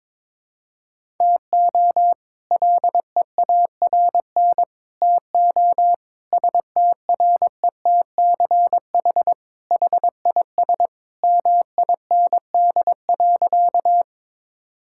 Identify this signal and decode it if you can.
Morse code, heard as a keyed tone, about 22 wpm, 700 Hz. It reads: TO LEARN TO STRETCH HIS MIND.
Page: morse-code-ninja-practice-22wpm